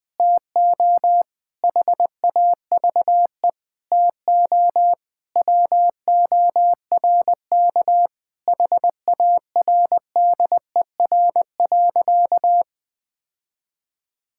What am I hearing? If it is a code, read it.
Morse code, heard as a keyed tone, about 20 wpm, 700 Hz. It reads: TO HAVE TO WORK HARDER.
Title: Sentences with 6-Word Phrases - 20wpm